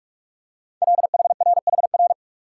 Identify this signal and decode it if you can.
Morse code, heard as a keyed tone, about 44 wpm, 700 Hz. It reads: LHRHR
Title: Code Groups - 5 Letter:Numbers - 45wpm